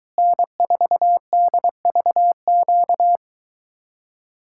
Morse code, heard as a keyed tone, about 23 wpm, 700 Hz. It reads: N4DVQ